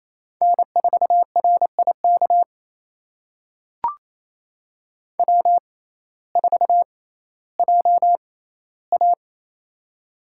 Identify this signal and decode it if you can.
Morse code, heard as a keyed tone, about 28 wpm, spaced as for 15 wpm, 700 Hz. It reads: N4RIK W4JA
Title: USA Call Signs - 6x Character Spacing - 28wpm